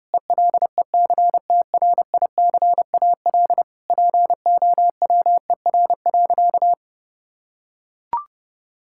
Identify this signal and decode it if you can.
Morse code, heard as a keyed tone, about 30 wpm, 700 Hz. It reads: ELECTRICAL POWER.
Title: Sentences from Top 1000 Words - Review Part 6 - 30wpm